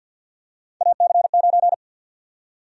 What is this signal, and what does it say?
Morse code, heard as a keyed tone, about 52 wpm, 700 Hz. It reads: AC9
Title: USA Call Sign Prefixes - 50wpm